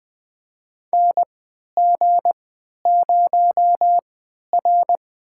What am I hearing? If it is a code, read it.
Morse code, heard as a keyed tone, about 20 wpm, spaced as for 11 wpm, 700 Hz. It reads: NG0R
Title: USA Call Signs - 3x Character Spacing - 20wpm